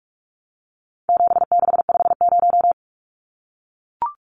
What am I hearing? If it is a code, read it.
Morse code, heard as a keyed tone, about 44 wpm, 700 Hz. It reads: 7650
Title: Numbers - 4-Digit 45wpm